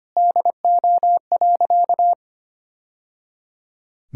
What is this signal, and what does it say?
Morse code, heard as a keyed tone, about 25 wpm, 700 Hz. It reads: DO.
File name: practice / 25wpm / Sentences with 2-Word Phrases - 25wpm